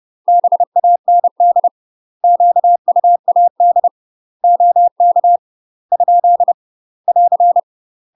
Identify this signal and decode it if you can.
Morse code, heard as a keyed tone, about 30 wpm, 700 Hz. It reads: BAND QUAD OK ? <AR>
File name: UK MASTER QSOs TRAINING / 30WPM / UK MASTER QSOs 0005